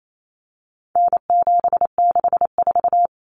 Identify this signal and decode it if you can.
Morse code, heard as a keyed tone, about 28 wpm, 700 Hz. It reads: N764
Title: Common Cut Numbers - 4-Digit 28wpm